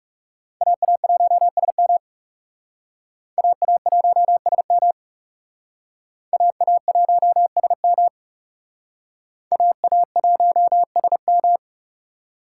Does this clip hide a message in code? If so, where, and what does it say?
Yes: Morse code from 0.6 to 11.6 s, heard as a keyed tone, about 39 wpm, 700 Hz. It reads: AA1SM AA1SM AA1SM AA1SM